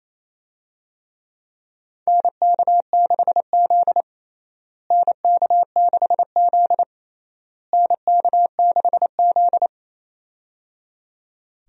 Morse code, heard as a keyed tone, about 28 wpm, 700 Hz. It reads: NK6Z NK6Z NK6Z
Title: USA Call Signs - 4 Characters - Repeated 3x - 28wpm